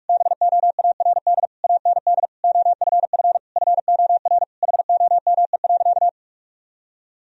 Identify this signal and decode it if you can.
Morse code, heard as a keyed tone, about 46 wpm, 700 Hz. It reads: BOARD AND OFF FOR HOME.
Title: Sentences from Top 1000 Words - Review Part 4 - 45wpm